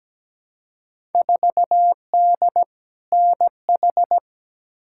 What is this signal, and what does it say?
Morse code, heard as a keyed tone, about 17 wpm, 700 Hz. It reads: 4D NH
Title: Field Day Exchanges Encore - 17wpm